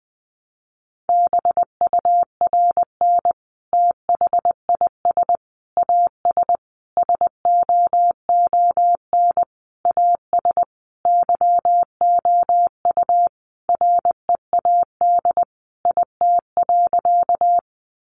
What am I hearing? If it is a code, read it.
Morse code, heard as a keyed tone, about 20 wpm, 700 Hz. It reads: BURN THIS AS SOON AS YOU READ IT.
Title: Sentences from Top 1300 Words 20wpm